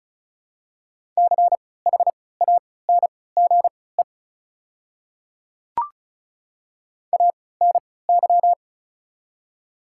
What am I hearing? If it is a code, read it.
Morse code, heard as a keyed tone, about 34 wpm, spaced as for 20 wpm, 700 Hz. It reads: CHANGE ANY
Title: 3x Character Spacing - Top 200 Words - 35wpm